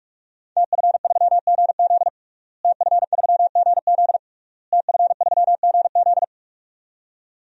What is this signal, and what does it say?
Morse code, heard as a keyed tone, about 48 wpm, 700 Hz. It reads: TF3GZ TF3GZ TF3GZ
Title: DXCC-100 Call Signs - 5 Characters - Repeated 3x - 45wpm